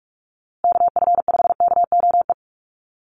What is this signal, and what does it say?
Morse code, heard as a keyed tone, about 44 wpm, 700 Hz. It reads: KF5KOE
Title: USA Call Signs Repeated 3x 45wpm